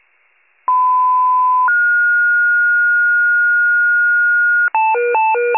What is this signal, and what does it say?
Two-tone page, heard as a one-second tone, then a three-second tone, about 1010 then 1430 Hz. It is heard from 0.7 to 4.7 s.